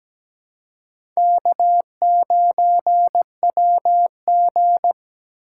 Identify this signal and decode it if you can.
Morse code, heard as a keyed tone, about 17 wpm, 700 Hz. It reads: K9WG
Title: USA Call Signs - 4 Characters - 4x Character Spacing - 17wpm